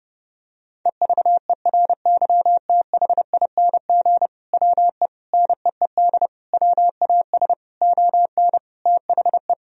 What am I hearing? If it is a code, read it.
Morse code, heard as a keyed tone, about 31 wpm, 700 Hz. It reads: EVERYTHING WE NEED WAS ON THE
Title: Sentences from Top 1000 Words - Review Part 1 - 30wpm